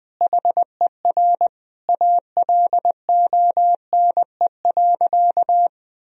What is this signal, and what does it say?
Morse code, heard as a keyed tone, about 19 wpm, 700 Hz. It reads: HER ALONE.
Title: Sentences from Top 1000 Words - Review Part 4 - 20wpm